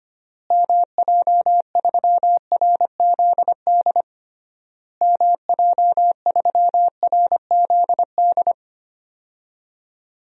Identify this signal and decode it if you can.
Morse code, heard as a keyed tone, about 25 wpm, 700 Hz. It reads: MJ3RZD MJ3RZD
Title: DXCC-100 Call Signs - 6 Characters - Repeated 3x - 25wpm